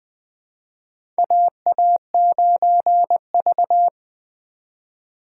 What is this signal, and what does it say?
Morse code, heard as a keyed tone, about 20 wpm, 700 Hz. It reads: AA9V
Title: USA Call Signs - 4 Characters - 3x Character Spacing - 20wpm